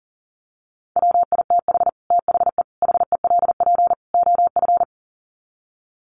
Morse code, heard as a keyed tone, about 40 wpm, 700 Hz. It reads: WITH THE HELP OF
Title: Sets of 4 Words 40wpm